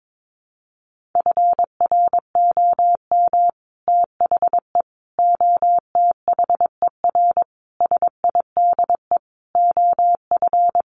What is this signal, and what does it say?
Morse code, heard as a keyed tone, about 22 wpm, 700 Hz. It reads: FROM THE OTHER SIDE OF
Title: Sets of 5 Words Encore 22wpm